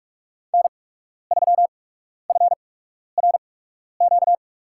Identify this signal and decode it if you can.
Morse code, heard as a keyed tone, about 41 wpm, spaced as for 12 wpm, 700 Hz. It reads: N3FRQ
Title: USA Call Signs - 5 Characters - 8x Character Spacing - 45wpm